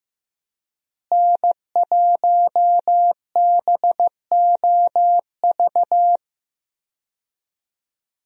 Morse code, heard as a keyed tone, about 15 wpm, 700 Hz. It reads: N1BOV
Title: USA Call Signs - 5 Characters - Repeated 2x - 15wpm